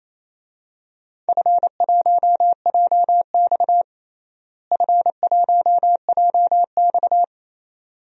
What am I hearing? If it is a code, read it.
Morse code, heard as a keyed tone, about 28 wpm, 700 Hz. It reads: F1JX F1JX
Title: DXCC-100 Call Signs - 4 Characters - Repeated 2x - 28wpm